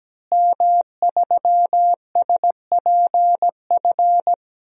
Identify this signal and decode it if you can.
Morse code, heard as a keyed tone, about 17 wpm, 700 Hz. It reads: M3SPF
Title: DXCC-100 Call Signs - 5 Characters - Repeated 3x - 17wpm